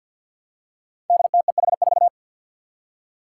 Morse code, heard as a keyed tone, about 48 wpm, 700 Hz. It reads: DTEH4
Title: Five Challenging Letter-Number Combinations 50wpm